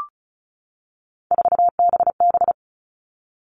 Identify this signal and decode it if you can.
Morse code, heard as a keyed tone, about 34 wpm, 700 Hz. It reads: 4BB
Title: All Cut Numbers - 3-Digit 35wpm